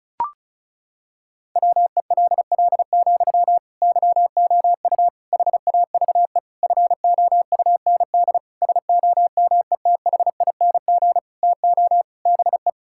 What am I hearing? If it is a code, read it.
Morse code, heard as a keyed tone, about 34 wpm, 700 Hz. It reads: WELL, YOU HAVE FOUND SOMETHING TO BE
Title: Sentences from Top 1000 Words - Review Part 6 - 35wpm